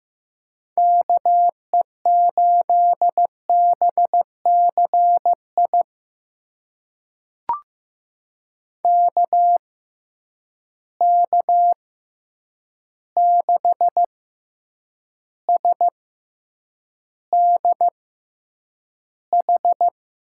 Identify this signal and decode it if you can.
Morse code, heard as a keyed tone, about 15 wpm, spaced as for 8 wpm, 700 Hz. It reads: KE8BCI KK6SDH
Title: USA Call Signs - 6 Characters - 6x Character Spacing - 15wpm